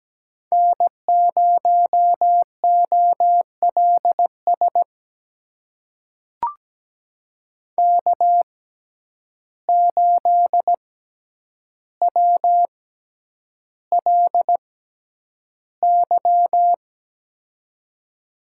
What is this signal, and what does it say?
Morse code, heard as a keyed tone, about 17 wpm, spaced as for 9 wpm, 700 Hz. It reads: N0OLS K8WLY
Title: USA Call Signs - 5 Characters - 6x Character Spacing - 17wpm